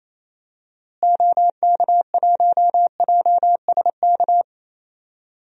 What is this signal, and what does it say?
Morse code, heard as a keyed tone, about 28 wpm, 700 Hz. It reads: OK1JSK
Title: DXCC-100 Call Signs - 6 Characters - Repeated 2x - 28wpm